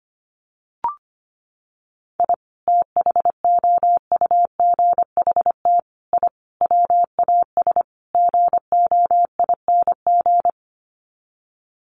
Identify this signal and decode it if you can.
Morse code, heard as a keyed tone, about 25 wpm, 700 Hz. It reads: I THOUGHT I WAS GOING
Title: Sets of 5 Words Encore2 25wpm